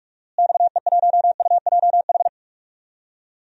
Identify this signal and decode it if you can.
Morse code, heard as a keyed tone, about 46 wpm, 700 Hz. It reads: XE1UJH
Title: DXCC-100 Call Signs - 6 Characters - Repeated 3x - 45wpm